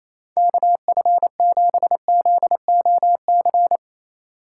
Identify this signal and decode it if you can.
Morse code, heard as a keyed tone, about 35 wpm, 700 Hz. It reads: KF7ZOC
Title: Speed Racing - USA 2x3 Call Signs 28wpm